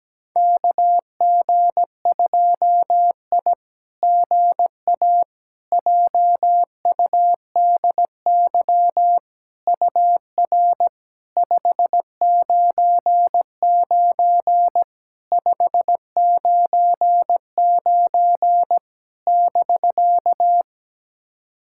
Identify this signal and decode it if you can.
Morse code, heard as a keyed tone, about 17 wpm, 700 Hz. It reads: KG2I GA JUDY UR 599 599 <BK>
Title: SOTA QSO Practice 17wpm